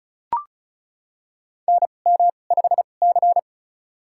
Morse code, heard as a keyed tone, about 34 wpm, spaced as for 25 wpm, 700 Hz. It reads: NM5C
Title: USA Call Signs - 4 Characters - 2x Character Spacing - 35wpm